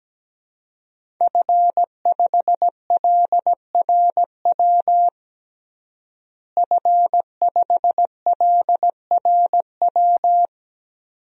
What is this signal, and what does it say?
Morse code, heard as a keyed tone, about 17 wpm, 700 Hz. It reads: F5LRW F5LRW
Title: DXCC-100 Call Signs - 5 Characters - Repeated 2x - 17wpm